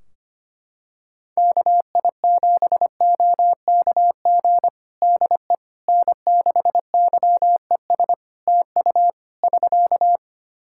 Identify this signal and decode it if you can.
Morse code, heard as a keyed tone, about 25 wpm, 700 Hz. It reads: KI7OKG DE N6YES TU <SK>
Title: Practice QSO - Exchange 4 - Ending - 25wpm